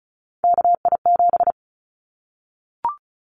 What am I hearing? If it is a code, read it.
Morse code, heard as a keyed tone, about 34 wpm, 700 Hz. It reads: KI7
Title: Callsign Prefixes for Top 25 DXCC - Repeated 3x - 35wpm